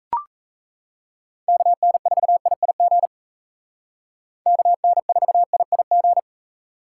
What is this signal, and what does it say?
Morse code, heard as a keyed tone, about 35 wpm, 700 Hz. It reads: KN4IIG KN4IIG